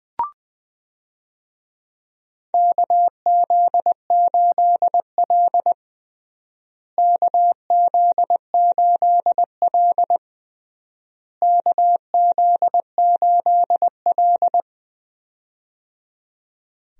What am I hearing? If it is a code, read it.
Morse code, heard as a keyed tone, about 20 wpm, 700 Hz. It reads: KZ8L KZ8L KZ8L
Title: USA Call Signs - 4 Characters - Repeated 3x - 20wpm